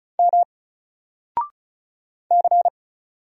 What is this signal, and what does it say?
Morse code, heard as a keyed tone, about 36 wpm, 700 Hz. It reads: M C
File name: Single-Letter-Number-35wpm